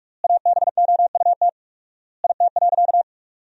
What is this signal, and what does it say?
Morse code, heard as a keyed tone, about 41 wpm, 700 Hz. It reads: ABOUT IT.
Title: 4x Word-Spacing - Sentences from Top 500 Words - 45wpm